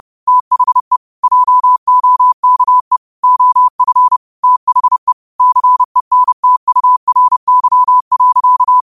Morse code, heard as a keyed tone, about 30 wpm, 1000 Hz. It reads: THE JOKE OF THE CENTURY.